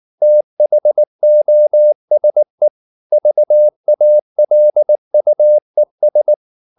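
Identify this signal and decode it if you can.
Morse code, heard as a keyed tone, about 19 wpm, 600 Hz. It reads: THOSE VALUES
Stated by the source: Morse code at 19 wpm from CW med Gnister, reading Ken Follett's novel Never in English.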